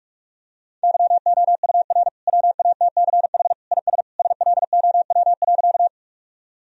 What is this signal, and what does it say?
Morse code, heard as a keyed tone, about 46 wpm, 700 Hz. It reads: YOUR WATCH IS SLOW.
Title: Sentences from Top 1000 Words - Review Part 2 - 45wpm